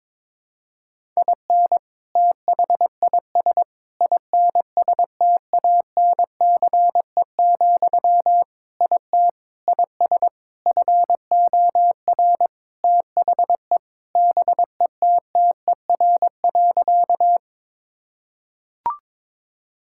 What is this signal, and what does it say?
Morse code, heard as a keyed tone, about 22 wpm, 700 Hz. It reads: IN THIS INSTANCE, IT IS FOR THE BETTER.
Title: Sentences from Top 1000 Words - Review Part 5 - 22wpm